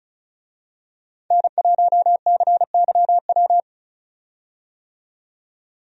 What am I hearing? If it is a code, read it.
Morse code, heard as a keyed tone, about 36 wpm, 700 Hz. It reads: N1CYW
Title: USA Call Signs - 5 Characters - 35wpm